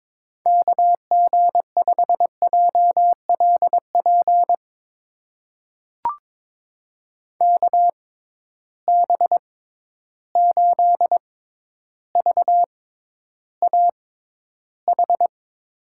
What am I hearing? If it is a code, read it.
Morse code, heard as a keyed tone, about 22 wpm, spaced as for 11 wpm, 700 Hz. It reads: KG5JLP KB8VAH
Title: USA Call Signs - 6 Characters - 6x Character Spacing - 22wpm